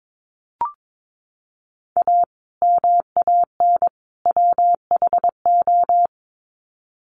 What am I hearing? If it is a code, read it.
Morse code, heard as a keyed tone, about 22 wpm, 700 Hz. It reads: A MAN WHO
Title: Sets of 3 Words 22wpm